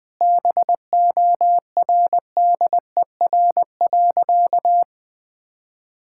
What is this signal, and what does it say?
Morse code, heard as a keyed tone, about 21 wpm, 700 Hz. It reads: BORDER.
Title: Sentences with 5-Word Phrases - 20wpm